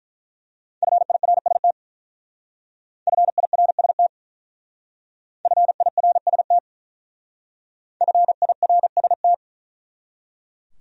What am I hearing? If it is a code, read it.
Morse code, heard as a keyed tone, about 42 wpm, 700 Hz. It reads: FIRST FIRST FIRST FIRST
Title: Speed-Racing Top 200 Words 35wpm